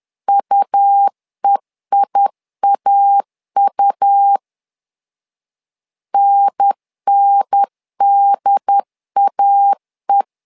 Morse code, heard as a keyed tone, about 10 wpm, 800 Hz. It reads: UEIAU NNDAE
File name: LEZIONI / 2 UHDN / 2 UHDN 10wpm 1904